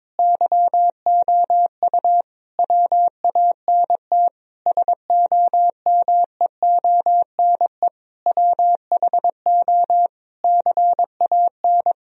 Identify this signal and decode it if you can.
Morse code, heard as a keyed tone, about 22 wpm, 700 Hz. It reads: YOU WANT SOMEONE WHO CAN
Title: Sentences from Top 1000 Words - Review Part 1 - 22wpm